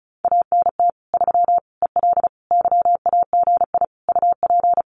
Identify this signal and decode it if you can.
Morse code, heard as a keyed tone, about 34 wpm, 700 Hz. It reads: ANT 3 EL YAGI UP
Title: Common QSO Phrases Encore 35wpm